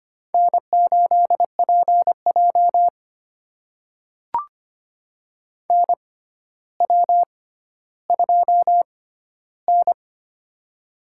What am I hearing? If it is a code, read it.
Morse code, heard as a keyed tone, about 25 wpm, spaced as for 13 wpm, 700 Hz. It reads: N8PJ NW2N
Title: USA Call Signs - 4 Characters - 6x Character Spacing - 25wpm